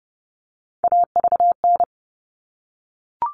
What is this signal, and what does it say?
Morse code, heard as a keyed tone, about 30 wpm, 700 Hz. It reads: AVN
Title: All Cut Numbers - 3-Digit 30wpm